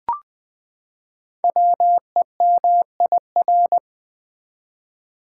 Morse code, heard as a keyed tone, about 20 wpm, 700 Hz. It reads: WEMIR
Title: Code Groups - 5 Letters - 20wpm